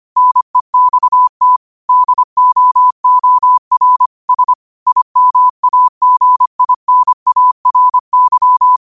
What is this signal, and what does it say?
Morse code, heard as a keyed tone, about 25 wpm, 1000 Hz. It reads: NEXT DOOR S IMAGINARY
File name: qotdfiles / 25wpm